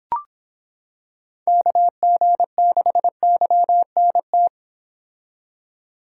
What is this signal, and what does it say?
Morse code, heard as a keyed tone, about 25 wpm, 700 Hz. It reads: KG6YNT